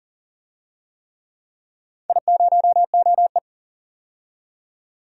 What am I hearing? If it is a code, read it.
Morse code, heard as a keyed tone, about 40 wpm, 700 Hz. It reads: I0OE
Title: DXCC-100 Call Signs - 4 Characters - 40wpm